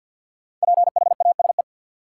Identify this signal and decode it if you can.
Morse code, heard as a keyed tone, about 52 wpm, 700 Hz. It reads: PHASE
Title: Sentences from Top 1200 Words - New Word Drill Alt - 50wpm